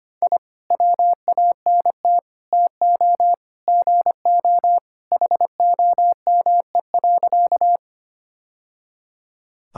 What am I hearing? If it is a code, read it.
Morse code, heard as a keyed tone, about 25 wpm, 700 Hz. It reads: I WANT TO GO HOME.
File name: Sentences with 3-Word Phrases - 25wpm